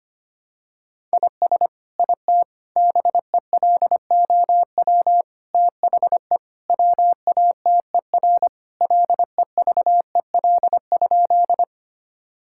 Morse code, heard as a keyed tone, about 25 wpm, 700 Hz. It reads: IS IT BELOW THE WATER LEVEL?
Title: Sentences from Top 1000 Words - Review Part 5 - 25wpm